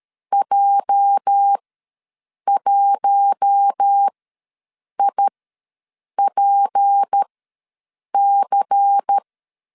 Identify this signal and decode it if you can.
Morse code, heard as a keyed tone, about 13 wpm, spaced as for 7 wpm, 800 Hz. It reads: J1IPC
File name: LEZIONI / 8 YJ45 / 8 YJ ws 12wpm 3342